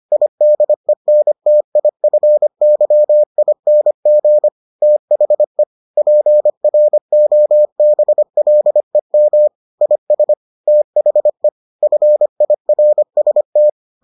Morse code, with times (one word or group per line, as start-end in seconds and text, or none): 0.1-4.5 IDENTIFYING
4.8-5.6 THE
6.0-9.5 PROBLEM
9.8-10.3 IS
10.7-11.5 THE
11.8-13.7 FIRST